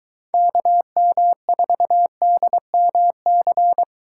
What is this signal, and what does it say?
Morse code, heard as a keyed tone, about 23 wpm, 700 Hz. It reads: KM4DMC